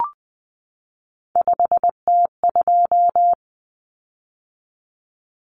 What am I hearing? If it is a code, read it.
Morse code, heard as a keyed tone, about 20 wpm, 700 Hz. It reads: 5T2
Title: Common Cut Numbers - 3-Digit 20wpm